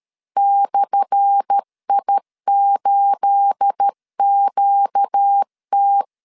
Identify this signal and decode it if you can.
Morse code, heard as a keyed tone, about 13 wpm, 800 Hz. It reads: /I8QT
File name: LEZIONI / 10 SIGNS / 10A SLASH 12wpm 2314